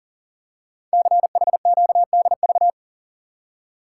Morse code, heard as a keyed tone, about 43 wpm, 700 Hz. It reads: CHQDV
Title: Code Groups - 5 Letter:Numbers - 40wpm